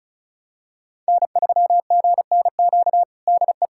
Morse code, heard as a keyed tone, about 36 wpm, 700 Hz. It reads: N3GNQ DE
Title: USA Callsign DE Callsign - 5 Characters - 35wpm